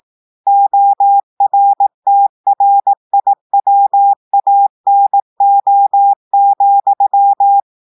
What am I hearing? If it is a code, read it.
Morse code, heard as a keyed tone, about 18 wpm, 800 Hz. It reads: ORTRIWANO,
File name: audios / portugais / portugais texte en clair 1 vitesse 18 mots minutes